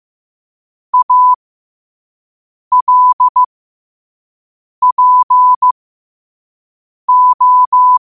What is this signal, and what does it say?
Morse code, heard as a keyed tone, about 15 wpm, spaced as for 5 wpm, 1000 Hz. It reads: ALPO